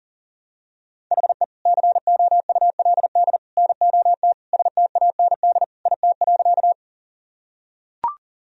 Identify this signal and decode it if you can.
Morse code, heard as a keyed tone, about 37 wpm, 700 Hz. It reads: HE COULD NOT STAND IT.
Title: Sentences from Top 1000 Words - Review Part 1 - 40wpm